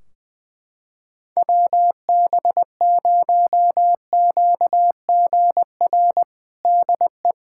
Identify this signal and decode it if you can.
Morse code, heard as a keyed tone, about 20 wpm, 700 Hz. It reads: WB0QGR DE
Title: Practice QSO - Exchange 4 - Ending - 20wpm